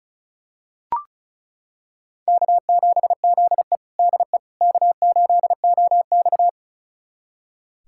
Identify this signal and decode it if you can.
Morse code, heard as a keyed tone, about 34 wpm, 700 Hz. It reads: K7ZE DE K8OX
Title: USA Callsign DE Callsign - 4 Characters - 35wpm